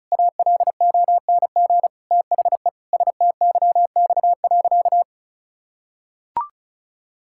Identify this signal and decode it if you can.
Morse code, heard as a keyed tone, about 36 wpm, 700 Hz. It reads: ALONG THE STYX.
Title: Pangram Challenge v2 - 35wpm